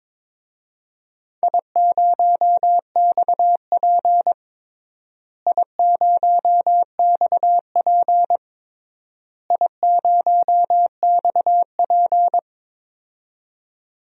Morse code, heard as a keyed tone, about 21 wpm, 700 Hz. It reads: I0XP I0XP I0XP
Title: DXCC-100 Call Signs - 4 Characters - Repeated 3x - 22wpm